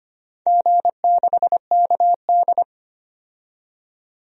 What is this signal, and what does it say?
Morse code, heard as a keyed tone, about 25 wpm, 700 Hz. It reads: G6KD
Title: DXCC-100 Call Signs - 4 Characters - 25wpm